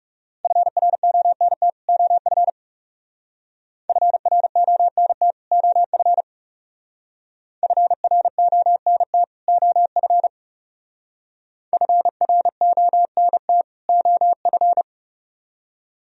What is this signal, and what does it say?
Morse code, heard as a keyed tone, about 35 wpm, 700 Hz. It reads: FRONT OF FRONT OF FRONT OF FRONT OF
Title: Speed-Racing - Sets of 2 Words 30wpm